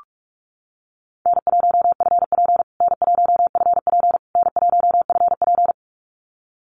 Morse code, heard as a keyed tone, about 44 wpm, 700 Hz. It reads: N1FP N1FP N1FP
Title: USA Call Signs Repeated 3x 45wpm